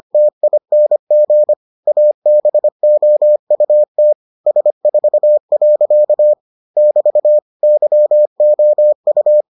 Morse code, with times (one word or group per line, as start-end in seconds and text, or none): 0.1-1.5 TING
1.9-4.1 ABOUT
4.5-6.3 S4.
6.8-7.4 <BT>
7.6-9.4 YOU